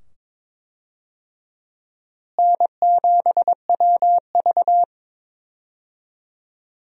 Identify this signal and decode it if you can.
Morse code, heard as a keyed tone, about 22 wpm, 700 Hz. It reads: N7WV
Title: USA Call Signs - 4 Characters - Repeated 2x - 22wpm